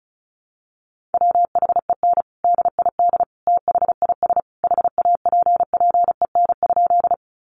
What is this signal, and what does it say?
Morse code, heard as a keyed tone, about 34 wpm, 700 Hz. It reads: WHEN DID THIS HAPPEN?
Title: Sentences from Top 400 Words 35wpm